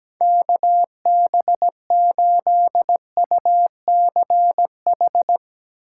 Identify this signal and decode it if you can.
Morse code, heard as a keyed tone, about 17 wpm, 700 Hz. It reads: KB8UCH
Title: USA Call Signs - 6 Characters - 4x Character Spacing - 17wpm